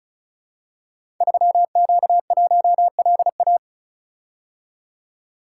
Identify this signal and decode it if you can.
Morse code, heard as a keyed tone, about 36 wpm, 700 Hz. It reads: 3Q1LA
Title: Code Groups - 5 Letter:Numbers - 35wpm